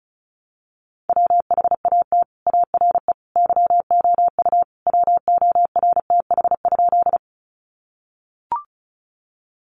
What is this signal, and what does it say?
Morse code, heard as a keyed tone, about 34 wpm, 700 Hz. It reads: WHAT ARE YOU WORTH?
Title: Sentences from Top 800 Words 35wpm